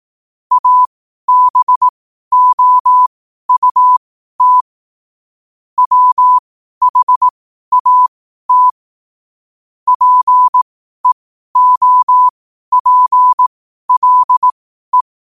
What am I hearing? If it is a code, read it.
Morse code, heard as a keyed tone, about 18 wpm, spaced as for 12 wpm, 1000 Hz. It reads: ABOUT WHAT PEOPLE